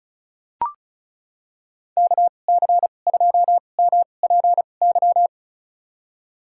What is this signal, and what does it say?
Morse code, heard as a keyed tone, about 36 wpm, spaced as for 25 wpm, 700 Hz. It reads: KC2MPY
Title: USA Call Signs - 2x Character Spacing - 35wpm